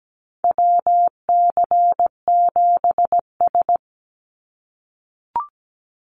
Morse code, heard as a keyed tone, about 17 wpm, 700 Hz. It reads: WC7S
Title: Call Signs - 17wpm